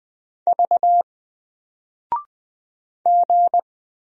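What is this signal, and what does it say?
Morse code, heard as a keyed tone, about 20 wpm, 700 Hz. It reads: V G
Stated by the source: Single letter and number Morse drill at 20 wpm, from N4LCD.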